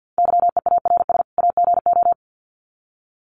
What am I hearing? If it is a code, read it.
Morse code, heard as a keyed tone, about 50 wpm, 700 Hz. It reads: YEARS AGO
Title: Sets of 2 Words 50wpm